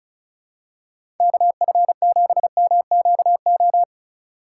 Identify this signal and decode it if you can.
Morse code, heard as a keyed tone, about 34 wpm, 700 Hz. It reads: KF7MQO
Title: USA Call Signs - 6 Characters - 4x Character Spacing - 35wpm